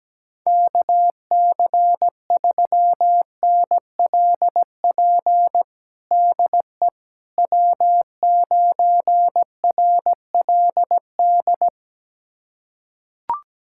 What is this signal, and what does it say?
Morse code, heard as a keyed tone, about 17 wpm, 700 Hz. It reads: KC3NLP DE W9RLD
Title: USA Callsign DE Callsign 17wpm